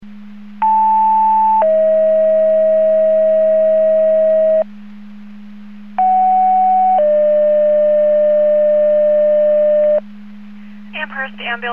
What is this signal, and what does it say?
Two-tone page, heard as a one-second tone, then a three-second tone, about 880 then 640 Hz, about 750 then 600 Hz, with speech. It is heard from 0.6 to 10.0 s.